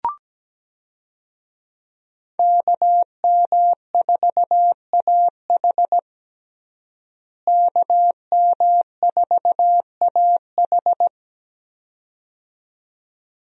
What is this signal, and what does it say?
Morse code, heard as a keyed tone, about 17 wpm, 700 Hz. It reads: KM4AH KM4AH
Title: USA Call Signs - 5 Characters - Repeated 2x - 17wpm